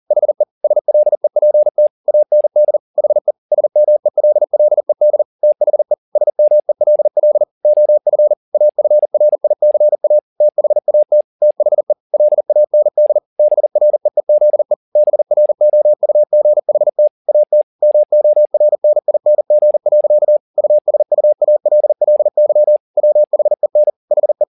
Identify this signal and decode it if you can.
Morse code, heard as a keyed tone, about 40 wpm, 600 Hz. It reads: HE SLEPT AND HE SMELLED THE SMELL OF AFRICA THAT THE LAND BREEZE BROUGHT AT MORNING. USUALLY WHEN HE